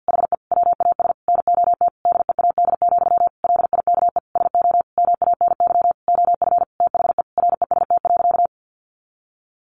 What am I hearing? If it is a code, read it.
Morse code, heard as a keyed tone, about 50 wpm, 700 Hz. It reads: HE WAS NOT DEAD, LIKE SO MANY OF THE REST.
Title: Sentences from Top 1500 Words 50wpm